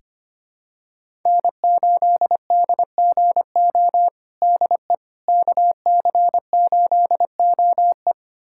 Morse code, heard as a keyed tone, about 25 wpm, 700 Hz. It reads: N8DGO DE KC8OE
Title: USA Callsign DE Callsign - 5 Characters - 25wpm